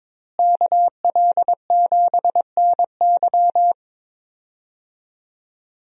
Morse code, heard as a keyed tone, about 22 wpm, 700 Hz. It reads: KL7NY
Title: USA Call Signs - 5 Characters - Repeated 3x - 22wpm